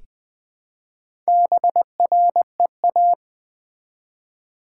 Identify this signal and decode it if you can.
Morse code, heard as a keyed tone, about 20 wpm, 700 Hz. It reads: BREA
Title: Build A Word - 1 Letter At A Time - 5-Letter Words - 20wpm